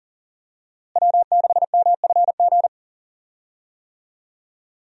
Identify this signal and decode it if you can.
Morse code, heard as a keyed tone, about 40 wpm, 700 Hz. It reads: W6MFG
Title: USA Call Signs - 5 Characters - Repeated 3x - 40wpm